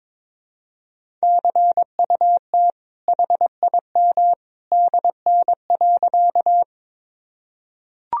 Morse code, heard as a keyed tone, about 22 wpm, 700 Hz. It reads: CUT HIM DN.
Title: Sentences from Top 300 Words and Abbreviations 22wpm